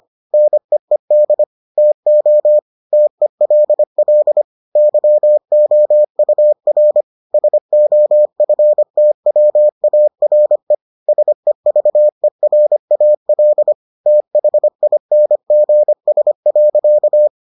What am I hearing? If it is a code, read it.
Morse code, heard as a keyed tone, about 25 wpm, 600 Hz. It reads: NEED TO TELL YOUR SOFTWARE SEVERAL THINGS.